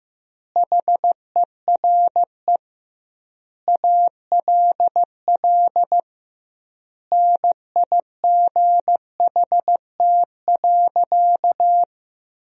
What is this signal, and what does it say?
Morse code, heard as a keyed tone, about 15 wpm, 700 Hz. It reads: HERE ALL NIGHT.
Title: 2x Word-Spacing - Sentences from Top 500 Words - 15wpm